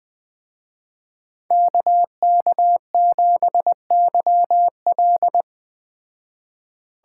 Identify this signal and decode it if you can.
Morse code, heard as a keyed tone, about 20 wpm, 700 Hz. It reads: KK7YL